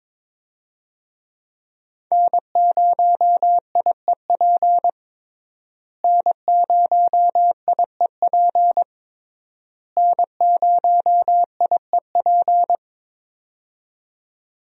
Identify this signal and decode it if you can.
Morse code, heard as a keyed tone, about 22 wpm, 700 Hz. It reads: N0IEP N0IEP N0IEP
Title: USA Call Signs - 5 Characters - Repeated 3x - 22wpm